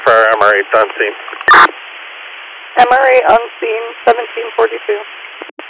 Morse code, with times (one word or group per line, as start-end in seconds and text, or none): none